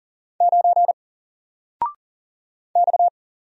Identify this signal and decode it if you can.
Morse code, heard as a keyed tone, about 38 wpm, 700 Hz. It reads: 9 X
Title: Single Letter-Number Alt 40wpm